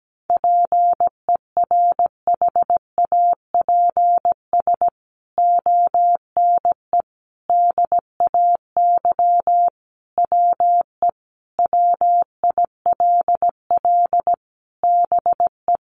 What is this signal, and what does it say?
Morse code, heard as a keyed tone, about 17 wpm, 700 Hz. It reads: PERHAPS ONE DAY WE WILL BE
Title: Sentences from Top 600 Words 17wpm